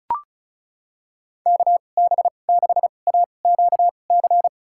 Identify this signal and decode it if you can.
Morse code, heard as a keyed tone, about 34 wpm, spaced as for 25 wpm, 700 Hz. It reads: KB6AQC
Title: USA Call Signs - 6 Characters - 2x Character Spacing - 35wpm